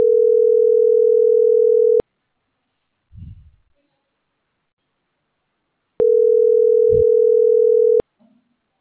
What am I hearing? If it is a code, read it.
Ringback tone (North America), 440 + 480 Hz.